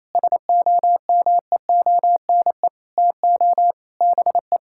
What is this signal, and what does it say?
Morse code, heard as a keyed tone, about 28 wpm, 700 Hz. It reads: SOMEONE TO BE
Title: Sentences from Top 1000 Words - Review Part 5 - 28wpm